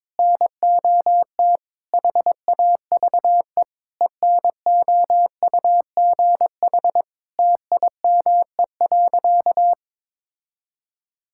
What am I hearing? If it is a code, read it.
Morse code, heard as a keyed tone, about 22 wpm, 700 Hz. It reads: NOT HAVE ENOUGH TIME.
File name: practice / 22wpm / Sentences with 4-Word Phrases - 22wpm